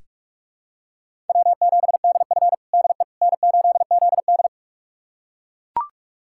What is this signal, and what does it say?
Morse code, heard as a keyed tone, about 41 wpm, 700 Hz. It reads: W7DF DE N8ZD
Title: USA Callsign DE Callsign - 4 Characters - 45wpm